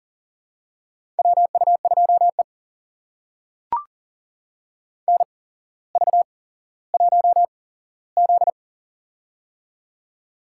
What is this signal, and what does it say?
Morse code, heard as a keyed tone, about 40 wpm, spaced as for 17 wpm, 700 Hz. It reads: WU2E NV1Z
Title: USA Call Signs - 4 Characters - 8x Character Spacing - 40wpm